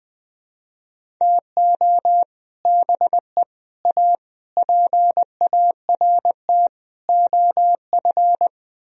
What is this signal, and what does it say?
Morse code, heard as a keyed tone, about 20 wpm, 700 Hz. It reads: TO BE A PART OF
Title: Speed-Racing - Sets of 5 Words 17wpm